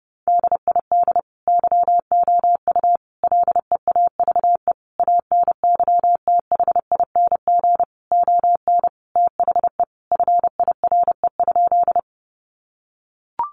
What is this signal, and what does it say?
Morse code, heard as a keyed tone, about 30 wpm, 700 Hz. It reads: DID YOU LEAVE ANYTHING ON THE FIRE?
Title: Sentences from Top 1000 Words 30wpm